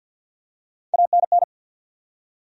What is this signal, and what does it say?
Morse code, heard as a keyed tone, about 52 wpm, 700 Hz. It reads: ANN
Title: USA Common Names - Max 3 Letters - 50wpm